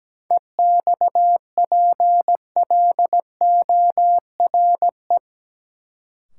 Morse code, heard as a keyed tone, about 17 wpm, 700 Hz. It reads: EXPLORE
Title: Sentences from Top 1100 Words - New Word Drill Alt - 17wpm